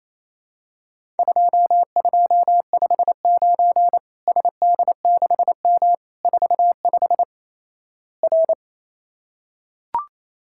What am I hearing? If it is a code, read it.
Morse code, heard as a keyed tone, about 27 wpm, 700 Hz. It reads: 2259 SD6M 45 R
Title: Worked All Europe DX Contest - QTC Exchange - 28wpm